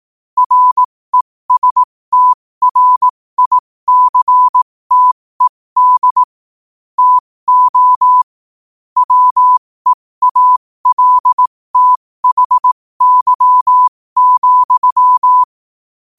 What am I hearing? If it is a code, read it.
Morse code, heard as a keyed tone, about 18 wpm, spaced as for 15 wpm, 1000 Hz. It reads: RESTRICTED TO WEALTHY,
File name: qotdfiles / 15wpm